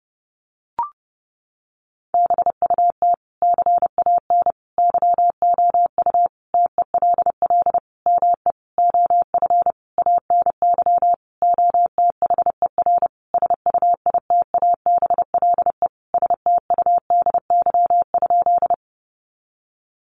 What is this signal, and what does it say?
Morse code, heard as a keyed tone, about 30 wpm, 700 Hz. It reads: BUT CAN YOU TELL ME OF ANY OTHER SUITABLE STUDY?
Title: Sentences from Top 1400 Words 30wpm